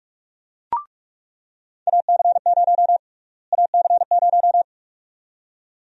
Morse code, heard as a keyed tone, about 46 wpm, 700 Hz. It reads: AC0 AC0
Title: USA Call Sign Prefixes - Repeated 2x - 45wpm